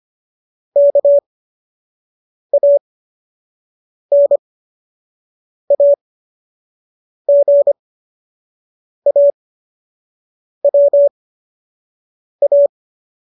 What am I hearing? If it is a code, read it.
Morse code, heard as a keyed tone, about 25 wpm, spaced as for 6 wpm, 570 Hz. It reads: KANAGAWA